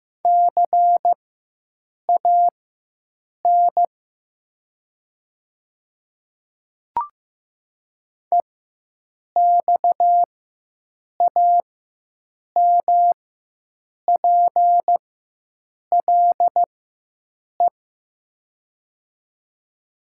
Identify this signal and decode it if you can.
Morse code, heard as a keyed tone, about 15 wpm, spaced as for 7 wpm, 700 Hz. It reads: CAN EXAMPLE
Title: 4x Character Spacing - Top 500 Words - 15wpm